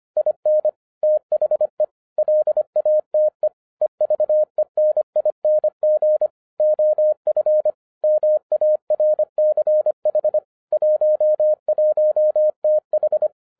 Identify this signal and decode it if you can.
Morse code, heard as a keyed tone, about 25 wpm, 610 Hz. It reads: IN THE LATE EVENING OF MARCH 11TH
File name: 2011-07-mp3 / n34 THE 3.11 2 25wpm